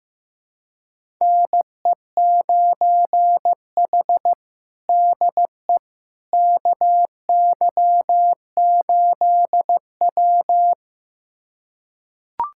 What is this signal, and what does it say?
Morse code, heard as a keyed tone, about 15 wpm, 700 Hz. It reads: NE9H DE KY8W
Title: USA Callsign DE Callsign - 4 Characters - 15wpm